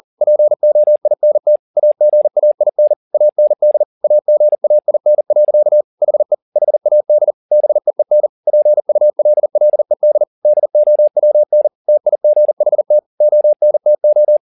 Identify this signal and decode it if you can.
Morse code, heard as a keyed tone, about 40 wpm, 600 Hz. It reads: POINT AGAIN AND AGAIN. HE HAD BEEN PULLED DOWN TIGHT ONTO
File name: the old man and the sea 2 40wpm